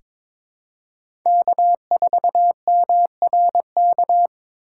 Morse code, heard as a keyed tone, about 22 wpm, 700 Hz. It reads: K4MRK
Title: USA Call Signs - 5 Characters - 2x Character Spacing - 22wpm